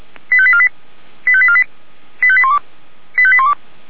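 Five-tone selective call (ZVEI), heard as quick runs of five tones, tones of about 70 ms. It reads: 85848 85848 85812 85812
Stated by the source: ZVEI five-tone selective call 89138 received by SDR, from fire brigade Feuerwehr Orlamünde's audio server.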